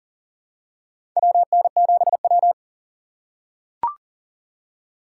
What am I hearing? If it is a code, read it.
Morse code, heard as a keyed tone, about 40 wpm, 700 Hz. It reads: WN7W
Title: USA Call Signs - 4 Characters - 4x Character Spacing - 40wpm